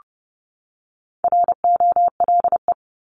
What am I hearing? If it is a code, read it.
Morse code, heard as a keyed tone, about 30 wpm, 700 Hz. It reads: ROLE
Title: Top 751-1000 Words 30wpm